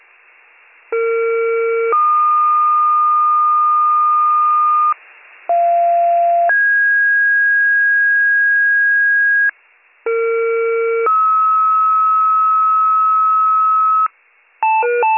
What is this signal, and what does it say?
Two-tone page, heard as a one-second tone, then a three-second tone, about 460 then 1220 Hz, about 690 then 1650 Hz, about 460 then 1290 Hz. It is heard from 0.9 to 14.1 s.